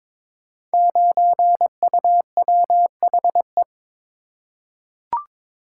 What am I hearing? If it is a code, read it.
Morse code, heard as a keyed tone, about 22 wpm, 700 Hz. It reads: 9UWHE
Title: Code Groups - 5 Letter:Numbers - 22wpm